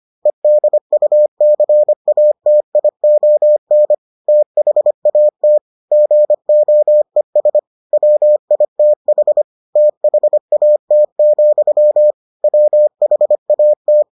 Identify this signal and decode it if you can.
Morse code, heard as a keyed tone, about 25 wpm, 600 Hz. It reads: EDUCATION THAT GOES WITH THAT, WHAT